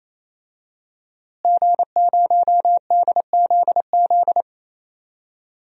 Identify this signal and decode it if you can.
Morse code, heard as a keyed tone, about 28 wpm, 700 Hz. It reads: G0DZZ
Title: DXCC-100 Call Signs - 5 Characters - Repeated 3x - 28wpm